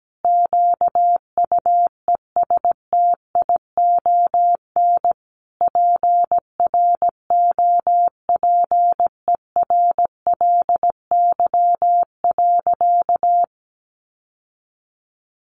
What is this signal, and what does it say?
Morse code, heard as a keyed tone, about 17 wpm, 700 Hz. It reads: QUESTION PROPERLY.
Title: Sentences from Top 1100 Words 17wpm